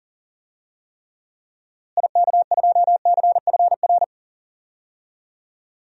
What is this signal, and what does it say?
Morse code, heard as a keyed tone, about 40 wpm, 700 Hz. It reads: IK2CFR
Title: DXCC-100 Call Signs - 6 Characters - Repeated 2x - 40wpm